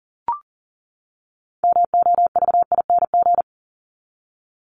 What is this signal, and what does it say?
Morse code, heard as a keyed tone, about 40 wpm, 700 Hz. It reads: MOVING
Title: ING Suffix - Top 100 Words with Base Words 40wpm